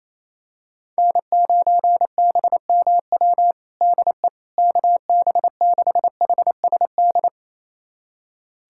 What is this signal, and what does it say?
Morse code, heard as a keyed tone, about 28 wpm, 700 Hz. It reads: N9BMW DE KB6HSD
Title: USA Callsign DE Callsign 28wpm